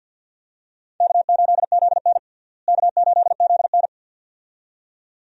Morse code, heard as a keyed tone, about 51 wpm, 700 Hz. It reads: K8ZN K8ZN
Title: USA Call Signs - 4 Characters - Repeated 2x - 50wpm